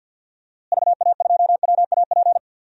Morse code, heard as a keyed tone, about 48 wpm, 700 Hz. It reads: VA2WAP
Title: DXCC-100 Call Signs - 6 Characters - Repeated 2x - 50wpm